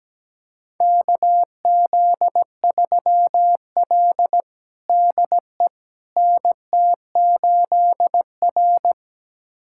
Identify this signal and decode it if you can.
Morse code, heard as a keyed tone, about 17 wpm, 700 Hz. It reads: KZ3L DE NT8R
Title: USA Callsign DE Callsign - 4 Characters - 17wpm